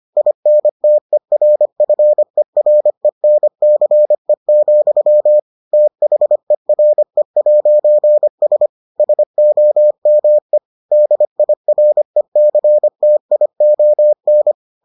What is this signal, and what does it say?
Morse code, heard as a keyed tone, about 25 wpm, 600 Hz. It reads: INTERFERENCE, THERE'S SOME DIRECTION